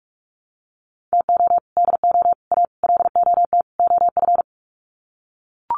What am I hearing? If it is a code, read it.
Morse code, heard as a keyed tone, about 44 wpm, 700 Hz. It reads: TO DO A LOT OF
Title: Sets of 5 Words Encore 45wpm